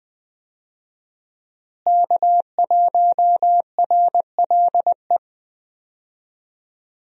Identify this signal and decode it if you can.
Morse code, heard as a keyed tone, about 20 wpm, 700 Hz. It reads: K1RLE
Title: USA Call Signs - 5 Characters - Repeated 3x - 20wpm